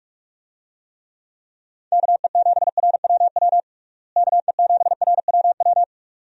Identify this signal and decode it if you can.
Morse code, heard as a keyed tone, about 46 wpm, 700 Hz. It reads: KE7RWW KE7RWW
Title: USA Call Signs - 6 Characters - Repeated 2x - 45wpm